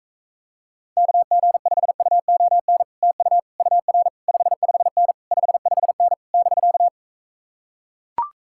Morse code, heard as a keyed tone, about 44 wpm, 700 Hz. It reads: KG5UON TU UR 55N 55N <BK>